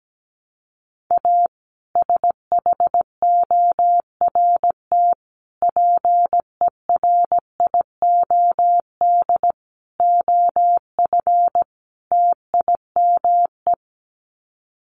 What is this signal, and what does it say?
Morse code, heard as a keyed tone, about 17 wpm, 700 Hz. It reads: A SHORT PERIOD OF TIME
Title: Sets of 5 Words Encore 17wpm